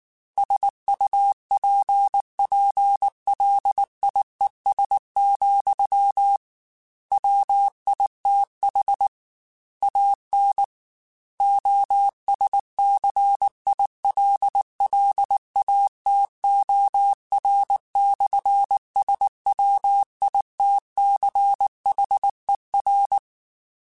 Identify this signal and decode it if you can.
Morse code, heard as a keyed tone, about 19 wpm, 800 Hz. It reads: SUPPLIES, WITH AN OSCILLATOR/SWITCHER
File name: Technical Subjects / Technical Subjects-mp3s / Sw Pwr Sup @18